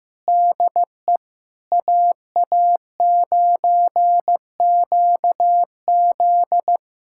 Morse code, heard as a keyed tone, about 15 wpm, 700 Hz. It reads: DE AA9QZ
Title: USA Callsign DE Callsign - 5 Characters - 15wpm